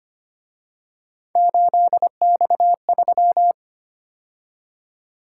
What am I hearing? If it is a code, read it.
Morse code, heard as a keyed tone, about 29 wpm, 700 Hz. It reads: 8X3
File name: Speed-Racing - Three Letter-Number Combinations - 25wpm